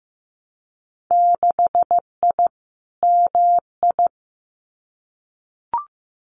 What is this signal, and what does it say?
Morse code, heard as a keyed tone, about 15 wpm, 700 Hz. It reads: 6I MI
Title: Winter Field Day Exchanges Encore 15wpm